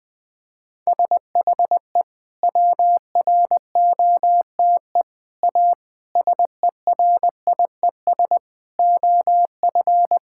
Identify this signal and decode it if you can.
Morse code, heard as a keyed tone, about 21 wpm, 700 Hz. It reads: SHE WROTE A SERIES OF
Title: Sentences with 3-Word Phrases - 20wpm